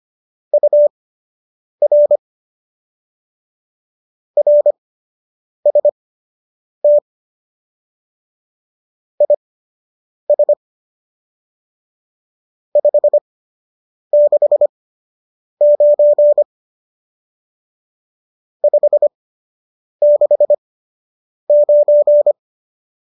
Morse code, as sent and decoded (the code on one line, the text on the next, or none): ..- .-. / .-. ... - / .. ... / ..... -.... ----. / ..... -.... ----.
UR RST IS 569 569